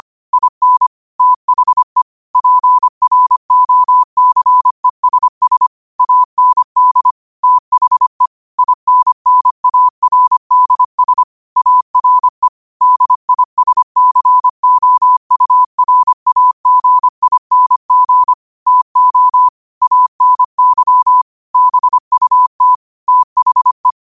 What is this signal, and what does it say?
Morse code, heard as a keyed tone, about 25 wpm, 1000 Hz. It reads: IN THE PROCESS AND THE INNARDS ARE DISCOURAGING TO ANY BUT THE